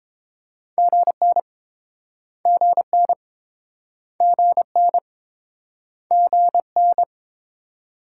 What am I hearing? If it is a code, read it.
Morse code, heard as a keyed tone, about 31 wpm, 700 Hz. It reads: GN GN GN GN